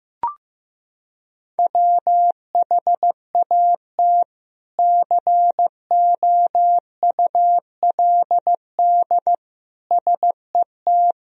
Morse code, heard as a keyed tone, about 15 wpm, 700 Hz. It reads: WHAT COULD SET
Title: Sentences from Top 200 Words and Abbreviations 15wpm